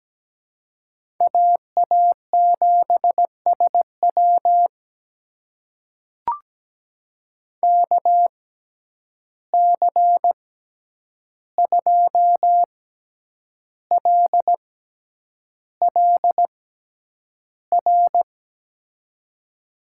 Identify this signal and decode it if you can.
Morse code, heard as a keyed tone, about 17 wpm, spaced as for 8 wpm, 700 Hz. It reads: AA7SW KC2LLR
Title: USA Call Signs - 6x Character Spacing - 17wpm